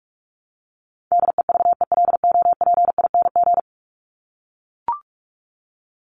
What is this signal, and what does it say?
Morse code, heard as a keyed tone, about 44 wpm, 700 Hz. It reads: DEVELOPING
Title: ING Suffix - Top 100 Words with Base Words 45wpm